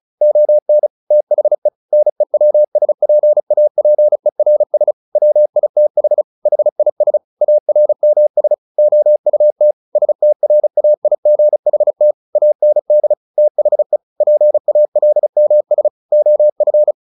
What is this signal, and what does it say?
Morse code, heard as a keyed tone, about 34 wpm, 600 Hz. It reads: ON THE NEWSPAPERS WITH HIS ARMS OUT STRAIGHT AND THE PALMS OF
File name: the old man and the sea 2 35wpm